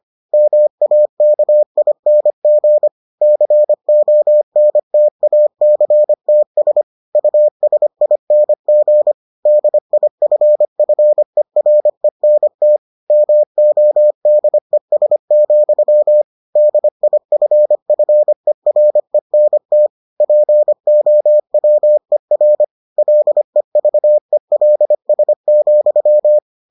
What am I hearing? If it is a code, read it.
Morse code, heard as a keyed tone, about 25 wpm, 600 Hz. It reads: MAKING CONTACTS USING DIFFERENT MODES, DIFFERENT POWER LEVELS,